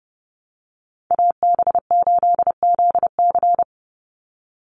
Morse code, heard as a keyed tone, about 30 wpm, 700 Hz. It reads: AB8ZC
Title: USA Call Signs 30wpm